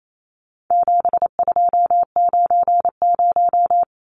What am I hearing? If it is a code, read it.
Morse code, heard as a keyed tone, about 28 wpm, 700 Hz. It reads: 7290
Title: Numbers - 4-Digit 28wpm